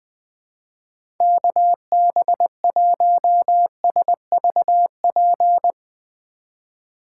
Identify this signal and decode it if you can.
Morse code, heard as a keyed tone, about 20 wpm, 700 Hz. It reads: KB1SVP
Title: USA Call Signs - 6 Characters - 2x Character Spacing - 20wpm